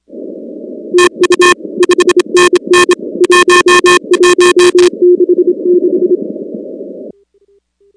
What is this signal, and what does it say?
Morse code, heard as a keyed tone, about 25 wpm, 360 Hz. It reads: TU 5NN 1166
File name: contests / 2019-cw-wpx-kt5j-k5pi- -2330 0-19